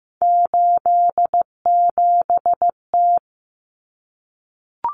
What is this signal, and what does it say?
Morse code, heard as a keyed tone, about 15 wpm, 700 Hz. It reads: 87T
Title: Common Cut Numbers - 3-Digit 15wpm